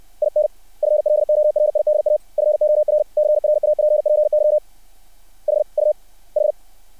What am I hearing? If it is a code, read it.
Morse code, heard as a keyed tone, about 60 wpm, spaced as for 52 wpm, 600 Hz. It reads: IN FLORIDA FOR CRAPPY HUH
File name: archived / CRSnet